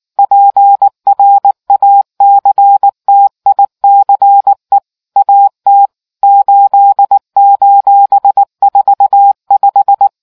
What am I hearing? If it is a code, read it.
Morse code, heard as a keyed tone, about 19 wpm, 800 Hz. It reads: PRACTICE AT 8:45